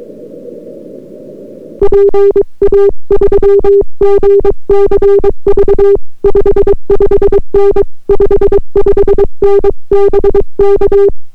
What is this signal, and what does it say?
Morse code, heard as a keyed tone, about 22 wpm, 400 Hz. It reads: PA3GCV 55N 55N BK